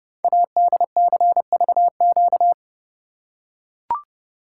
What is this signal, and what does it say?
Morse code, heard as a keyed tone, about 28 wpm, 700 Hz. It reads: ADCVQ
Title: Code Groups - 5 Letters - 30wpm